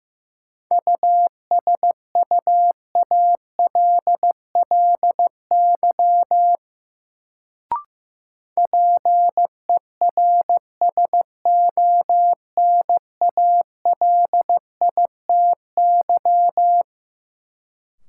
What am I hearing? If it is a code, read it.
Morse code, heard as a keyed tone, about 15 wpm, 700 Hz. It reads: USUALLY PERSONALITY
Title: Sentences from Top 1400 Words - New Word Drill Alt - 15wpm